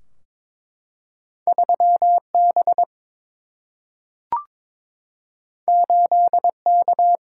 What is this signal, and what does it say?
Morse code, heard as a keyed tone, about 22 wpm, 700 Hz. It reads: 3 B 8 K
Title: Single-Letter Single-Number 22wpm